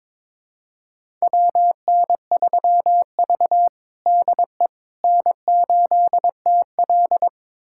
Morse code, heard as a keyed tone, about 21 wpm, 700 Hz. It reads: WN3V DE N8TL
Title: USA Callsign DE Callsign - 4 Characters - 22wpm